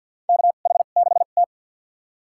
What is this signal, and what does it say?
Morse code, heard as a keyed tone, about 52 wpm, spaced as for 37 wpm, 700 Hz. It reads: KH6T
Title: USA Call Signs - 4 Characters - 2x Character Spacing - 50wpm